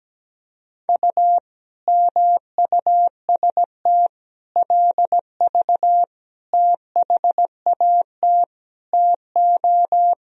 Morse code, heard as a keyed tone, about 17 wpm, 700 Hz. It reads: U MUST LV THAT TO
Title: Sentences from Top 400 Words and Abbreviations 17wpm